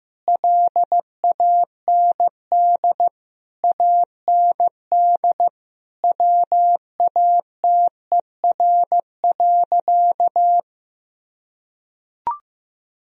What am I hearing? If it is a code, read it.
Morse code, heard as a keyed tone, about 15 wpm, 700 Hz. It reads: LAND AND WATER.
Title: Sentences from Top 1000 Words - Review Part 1 - 15wpm